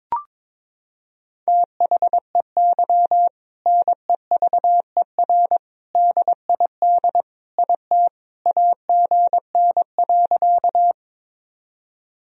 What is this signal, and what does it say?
Morse code, heard as a keyed tone, about 22 wpm, 700 Hz. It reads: THEY NEVER DID IT AGN.
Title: Sentences from Top 300 Words and Abbreviations 22wpm